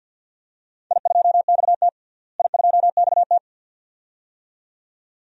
Morse code, heard as a keyed tone, about 51 wpm, 700 Hz. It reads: I2XT I2XT
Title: DXCC-100 Call Signs - 4 Characters - Repeated 2x - 50wpm